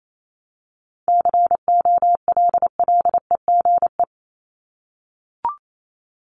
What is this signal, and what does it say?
Morse code, heard as a keyed tone, about 28 wpm, 700 Hz. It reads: COLLEGE
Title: Top 1001-1250 Words 28wpm